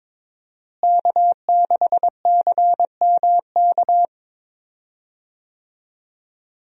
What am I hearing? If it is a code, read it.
Morse code, heard as a keyed tone, about 23 wpm, 700 Hz. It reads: K6CMK
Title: USA Call Signs - 5 Characters - Repeated 3x - 22wpm